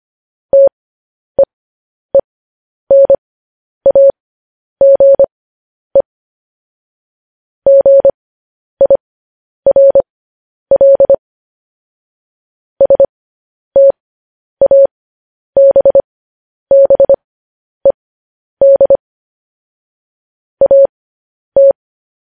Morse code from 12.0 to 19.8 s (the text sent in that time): STABBED